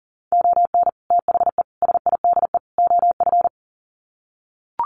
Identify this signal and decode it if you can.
Morse code, heard as a keyed tone, about 40 wpm, 700 Hz. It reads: ON THE SIDE OF
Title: Sets of 4 Words 40wpm